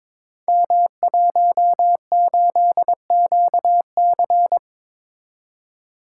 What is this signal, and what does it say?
Morse code, heard as a keyed tone, about 23 wpm, 700 Hz. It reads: M18QC
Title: Code Groups - 5 Letter:Numbers - 22wpm